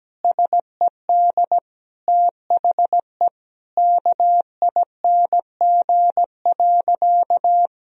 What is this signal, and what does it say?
Morse code, heard as a keyed tone, about 17 wpm, 700 Hz. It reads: SED THE KING.
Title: Sentences from Top 400 Words and Abbreviations 17wpm